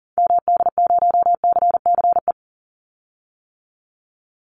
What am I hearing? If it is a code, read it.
Morse code, heard as a keyed tone, about 38 wpm, 700 Hz. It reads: MD0CCE
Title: Call Signs - 40wpm